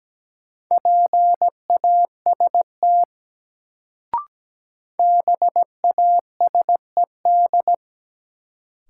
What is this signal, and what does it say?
Morse code, heard as a keyed tone, about 17 wpm, 700 Hz. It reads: PAST BASED
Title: Sets of 2 Words - New Word Drill - 17wpm